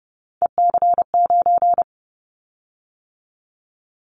Morse code, heard as a keyed tone, about 30 wpm, 700 Hz. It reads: EC9
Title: Callsign Prefixes for Top 100 DXCC - 30wpm